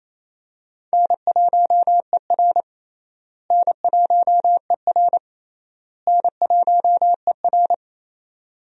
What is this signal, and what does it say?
Morse code, heard as a keyed tone, about 28 wpm, 700 Hz. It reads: N1ER N1ER N1ER
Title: USA Call Signs - 4 Characters - Repeated 3x - 28wpm